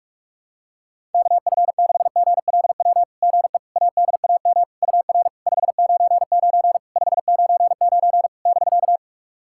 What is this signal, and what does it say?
Morse code, heard as a keyed tone, about 41 wpm, 690 Hz. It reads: KF6GLW GE ADAM UR 599 599 <BK>
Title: SOTA QSO Practice 45wpm